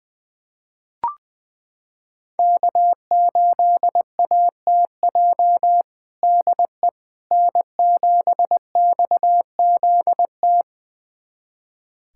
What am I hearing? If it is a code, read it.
Morse code, heard as a keyed tone, about 20 wpm, 700 Hz. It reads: K8ATJ DE N7XZT
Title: USA Callsign DE Callsign - 5 Characters - 20wpm